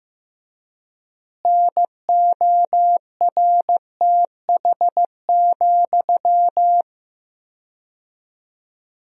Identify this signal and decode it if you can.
Morse code, heard as a keyed tone, about 15 wpm, 700 Hz. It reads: NORTH,
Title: 4x Word-Spacing - Sentences from Top 400 Words - 15wpm